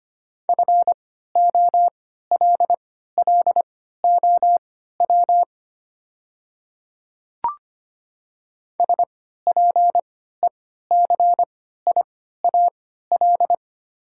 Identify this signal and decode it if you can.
Morse code, heard as a keyed tone, about 25 wpm, spaced as for 14 wpm, 700 Hz. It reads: FOLLOW SPECIAL
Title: 3x Character Spacing - Top 500 Words - 25wpm